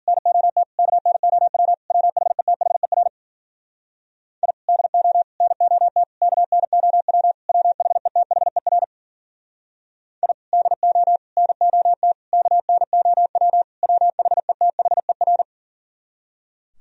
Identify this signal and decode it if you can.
Morse code, heard as a keyed tone, about 52 wpm, 700 Hz. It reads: NOT KNOW WHETHER I DO NOT KNOW WHETHER I DO NOT KNOW WHETHER